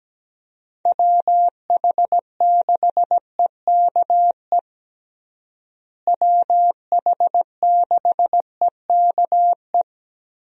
Morse code, heard as a keyed tone, about 17 wpm, 700 Hz. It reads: WH6EKE WH6EKE
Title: USA Call Signs - 6 Characters - Repeated 2x - 17wpm